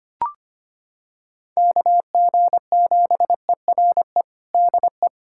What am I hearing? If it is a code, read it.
Morse code, heard as a keyed tone, about 25 wpm, 700 Hz. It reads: KG7ERE DE
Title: USA Callsign DE Callsign 25wpm